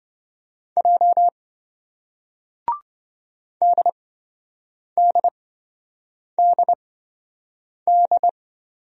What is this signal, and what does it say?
Morse code, heard as a keyed tone, about 27 wpm, spaced as for 7 wpm, 700 Hz. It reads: J DDDD